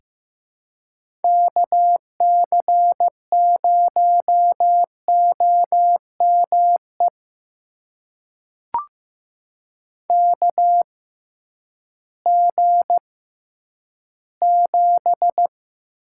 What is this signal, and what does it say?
Morse code, heard as a keyed tone, about 15 wpm, spaced as for 10 wpm, 700 Hz. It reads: KC0OME KG7
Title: USA Call Signs - 6 Characters - 6x Character Spacing - 15wpm